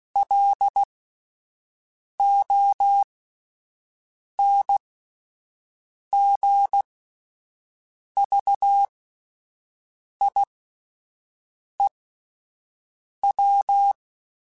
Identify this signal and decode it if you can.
Morse code, heard as a keyed tone, about 16 wpm, spaced as for 5 wpm, 800 Hz. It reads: LONGVIEW